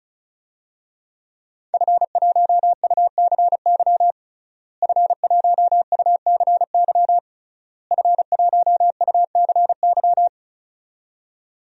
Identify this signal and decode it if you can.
Morse code, heard as a keyed tone, about 36 wpm, 700 Hz. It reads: F1UCY F1UCY F1UCY
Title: DXCC-100 Call Signs - 5 Characters - Repeated 3x - 35wpm